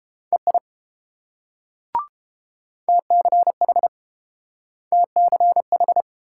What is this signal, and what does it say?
Morse code, heard as a keyed tone, about 29 wpm, 700 Hz. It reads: EI TCH TCH